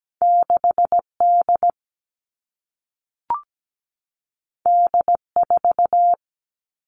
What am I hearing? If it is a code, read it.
Morse code, heard as a keyed tone, about 17 wpm, 700 Hz. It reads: 6D D4
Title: All Cut Numbers - 2-Digit 17wpm